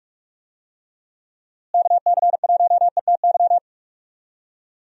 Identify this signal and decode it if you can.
Morse code, heard as a keyed tone, about 42 wpm, 700 Hz. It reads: KC1ETY
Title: USA Call Signs - 6 Characters - Repeated 3x - 45wpm